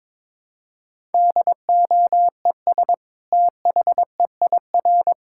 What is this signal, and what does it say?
Morse code, heard as a keyed tone, about 22 wpm, 700 Hz. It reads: DOES THEIR
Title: Sentences from Top 1000 Words - Review Part 5 - 22wpm